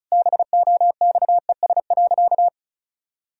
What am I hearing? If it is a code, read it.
Morse code, heard as a keyed tone, about 36 wpm, 700 Hz. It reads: BOXES.
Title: Pangram Challenge v2 - 35wpm